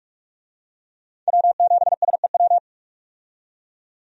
Morse code, heard as a keyed tone, about 46 wpm, 700 Hz. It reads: W7SEW
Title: USA Call Signs - 5 Characters - 45wpm